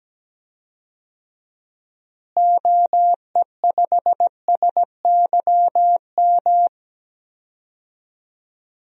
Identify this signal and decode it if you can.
Morse code, heard as a keyed tone, about 17 wpm, 700 Hz. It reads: OE5SYM
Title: DXCC-100 Call Signs - 6 Characters - Repeated 2x - 17wpm